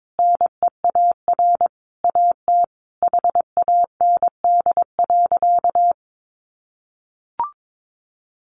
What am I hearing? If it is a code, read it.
Morse code, heard as a keyed tone, about 22 wpm, 700 Hz. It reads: NEAR AT HAND.
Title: Sentences from Top 1500 Words 22wpm